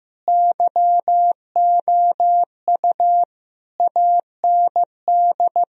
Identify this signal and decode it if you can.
Morse code, heard as a keyed tone, about 15 wpm, 700 Hz. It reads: YOU AND
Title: Sentences from Top 1000 Words - Review Part 1 - 15wpm